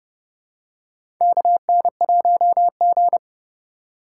Morse code, heard as a keyed tone, about 32 wpm, 700 Hz. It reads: KN1G
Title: USA Call Signs - 2x Character Spacing - 30wpm